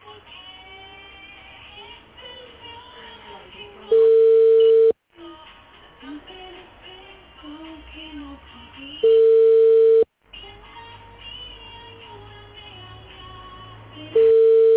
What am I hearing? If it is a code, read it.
Ringback tone (China, 450 Hz), with speech.